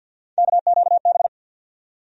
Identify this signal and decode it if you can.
Morse code, heard as a keyed tone, about 51 wpm, 700 Hz. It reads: KQB
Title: USA Call Sign Postfixes - 3 Letters 50wpm